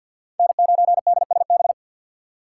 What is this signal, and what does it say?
Morse code, heard as a keyed tone, about 52 wpm, 700 Hz. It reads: N9DSB
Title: USA Call Signs - 5 Characters - 3x Character Spacing - 50wpm